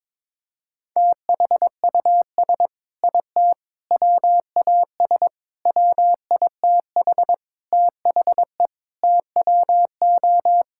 Morse code, heard as a keyed tone, about 22 wpm, 700 Hz. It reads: THUS IT WAS WITH THE TWO
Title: Sentences from Top 1000 Words - Review Part 5 - 22wpm